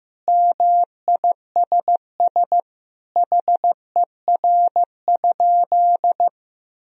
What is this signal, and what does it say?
Morse code, heard as a keyed tone, about 15 wpm, 700 Hz. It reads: MISS HER?
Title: Sentences from Top 1000 Words - Review Part 3 - 15wpm